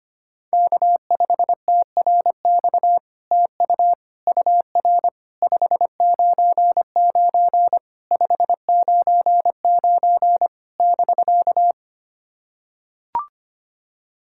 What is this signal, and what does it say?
Morse code, heard as a keyed tone, about 25 wpm, 700 Hz. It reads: K5TRX TU UR 599 599 <BK>
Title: SOTA QSO Practice 25wpm